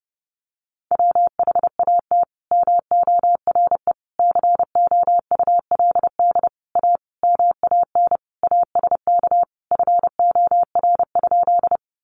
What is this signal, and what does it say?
Morse code, heard as a keyed tone, about 30 wpm, 700 Hz. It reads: WHAT MORE COULD A MAN ASK FOR?
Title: Sentences from Top 200 Words 30wpm